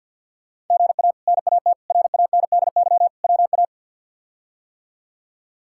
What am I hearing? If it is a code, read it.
Morse code, heard as a keyed tone, about 47 wpm, 700 Hz. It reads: GA NAT RANDY WA
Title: K1USN SST QSO Practice - 50wpm